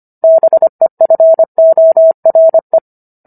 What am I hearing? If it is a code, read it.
Morse code, heard as a keyed tone, about 25 wpm, 650 Hz. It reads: BEFORE